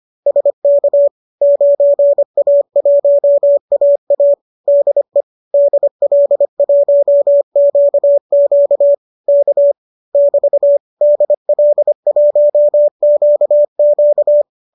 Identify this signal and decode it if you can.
Morse code, heard as a keyed tone, about 25 wpm, 570 Hz. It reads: SK 9A1AA DE DL1QQ K <BT> DL1QQ